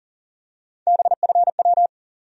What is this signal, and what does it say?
Morse code, heard as a keyed tone, about 40 wpm, 700 Hz. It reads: BFW
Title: USA Call Sign Postfixes - 3 Letters 40wpm